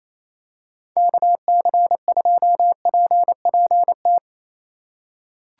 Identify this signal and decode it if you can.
Morse code, heard as a keyed tone, about 27 wpm, 700 Hz. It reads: KC2PPT
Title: USA Call Signs - 6 Characters - 4x Character Spacing - 28wpm